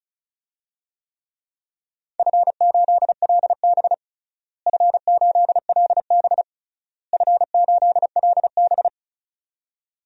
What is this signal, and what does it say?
Morse code, heard as a keyed tone, about 34 wpm, 700 Hz. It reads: F8LB F8LB F8LB
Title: DXCC-100 Call Signs - 4 Characters - Repeated 3x - 35wpm